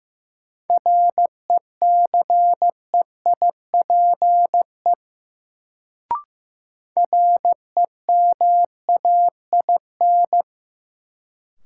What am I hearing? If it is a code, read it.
Morse code, heard as a keyed tone, about 15 wpm, 700 Hz. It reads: RECEIPE REMAIN
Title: Sentences from Top 1200 Words - New Word Drill Alt - 15wpm